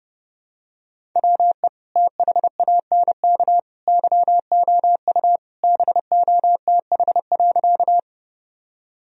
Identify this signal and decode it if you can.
Morse code, heard as a keyed tone, about 30 wpm, 700 Hz. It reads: WE THANK YOU BOTH.
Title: Sentences from Top 1000 Words - Review Part 1 - 30wpm